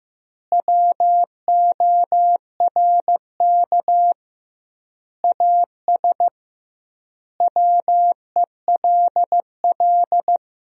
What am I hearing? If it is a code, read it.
Morse code, heard as a keyed tone, about 15 wpm, 700 Hz. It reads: WORK AS WELL
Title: 2x Word-Spacing - Sentences from Top 500 Words - 15wpm